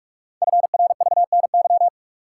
Morse code, heard as a keyed tone, about 44 wpm, 700 Hz. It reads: FRVNY
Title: Code Groups - 5 Letters - 45wpm